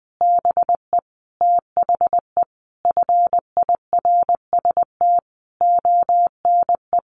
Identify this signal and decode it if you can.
Morse code, heard as a keyed tone, about 20 wpm, 700 Hz. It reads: BE THE FIRST ONE
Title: Sentences from Top 1300 Words 20wpm